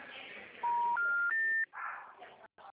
Special information tone, with noise.